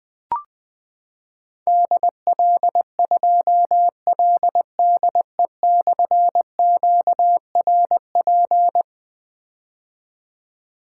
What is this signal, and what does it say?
Morse code, heard as a keyed tone, about 20 wpm, 700 Hz. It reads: DL2LDE/QRP
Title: Difficult Call Signs 20wpm